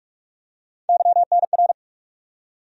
Morse code, heard as a keyed tone, about 46 wpm, 700 Hz. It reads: YNR
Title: USA Call Sign Postfixes - 3 Letters 45wpm